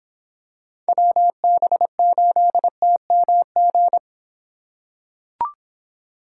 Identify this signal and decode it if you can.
Morse code, heard as a keyed tone, about 26 wpm, 700 Hz. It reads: WB8TMG